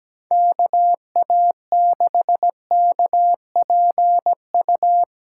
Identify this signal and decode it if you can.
Morse code, heard as a keyed tone, about 17 wpm, 700 Hz. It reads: KA6KPU
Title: USA Call Signs - 2x Character Spacing - 17wpm